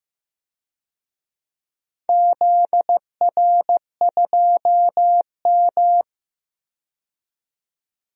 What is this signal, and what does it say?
Morse code, heard as a keyed tone, about 15 wpm, 700 Hz. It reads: ZR2M
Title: DXCC-100 Call Signs - 4 Characters - 15wpm